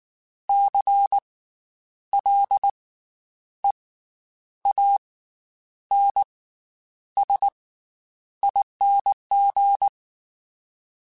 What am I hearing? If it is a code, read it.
Morse code, heard as a keyed tone, about 19 wpm, spaced as for 9 wpm, 800 Hz. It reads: CLEANSING
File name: Special INGs / Special INGs-mp3s / Slow ing @18